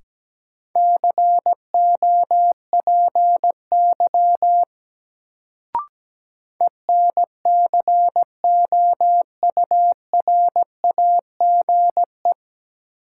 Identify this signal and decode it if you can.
Morse code, heard as a keyed tone, about 17 wpm, 700 Hz. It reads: COPY ENCOURAGE
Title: Sentences from Top 1400 Words - New Word Drill - 17wpm